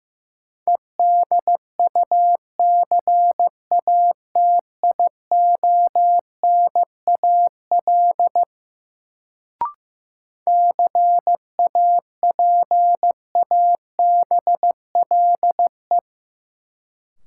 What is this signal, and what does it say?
Morse code, heard as a keyed tone, about 15 wpm, 700 Hz. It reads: EDUCATIONAL CAPABLE
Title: Sentences from Top 1200 Words - New Word Drill - 15wpm